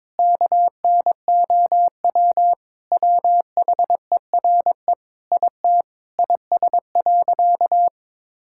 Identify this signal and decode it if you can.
Morse code, heard as a keyed tone, about 22 wpm, 700 Hz. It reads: KNOW WHERE IT IS.
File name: practice / 22wpm / Sentences with 4-Word Phrases - 22wpm